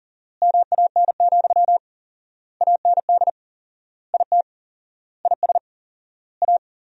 Morse code, heard as a keyed tone, about 40 wpm, 700 Hz. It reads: MAN, AND IT IS A
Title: 4x Word-Spacing - Sentences from Top 200 Words - 40wpm